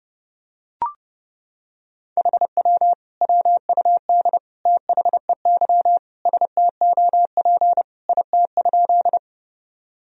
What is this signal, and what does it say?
Morse code, heard as a keyed tone, about 32 wpm, 700 Hz. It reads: HW WUD THEY STOP IT?
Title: Sentences from Top 300 Words and Abbreviations 30wpm